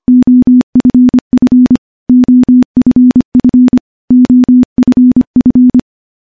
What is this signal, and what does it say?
Morse code, heard as a keyed tone, about 25 wpm, 250 Hz. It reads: OFF OFF OFF
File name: morse-training-25WPM